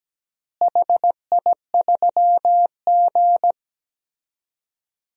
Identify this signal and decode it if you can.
Morse code, heard as a keyed tone, about 17 wpm, 700 Hz. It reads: HI3G
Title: DXCC-100 Call Signs - 4 Characters - Repeated 2x - 17wpm